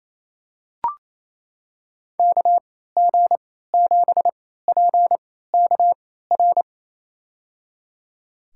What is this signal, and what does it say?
Morse code, heard as a keyed tone, about 28 wpm, spaced as for 16 wpm, 700 Hz. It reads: KG7PKR
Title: USA Call Signs - 3x Character Spacing - 28wpm